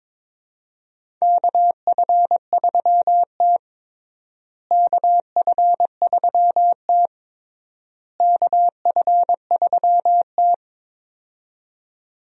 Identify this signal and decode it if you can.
Morse code, heard as a keyed tone, about 22 wpm, 700 Hz. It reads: KF3T KF3T KF3T
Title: USA Call Signs - 4 Characters - Repeated 3x - 22wpm